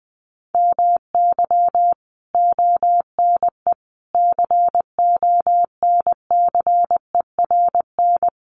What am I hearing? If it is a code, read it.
Morse code, heard as a keyed tone, about 20 wpm, 700 Hz. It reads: MY ONE CONCERN
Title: Sentences from Top 1200 Words 20wpm